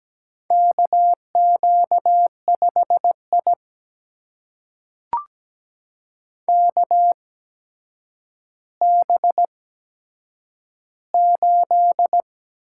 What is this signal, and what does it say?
Morse code, heard as a keyed tone, about 17 wpm, spaced as for 8 wpm, 700 Hz. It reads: KQ5I KB8
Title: USA Call Signs - 8x Character Spacing - 17wpm